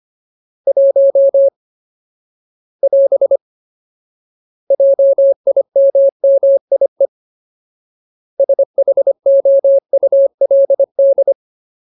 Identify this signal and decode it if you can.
Morse code, heard as a keyed tone, about 25 wpm, 550 Hz. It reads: 1 <AS> JIMMIE SHOULD
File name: Ultimate Word List / 25WPM Ultimate